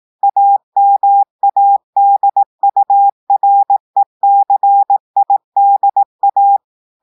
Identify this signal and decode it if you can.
Morse code, heard as a keyed tone, about 18 wpm, 800 Hz. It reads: AMADURECIDA
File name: audios / portugais / portugais texte en clair 1 vitesse 18 mots minutes